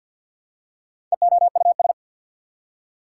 Morse code, heard as a keyed tone, about 53 wpm, 690 Hz. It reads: EOUS
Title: Prefixes and Suffixes 50wpm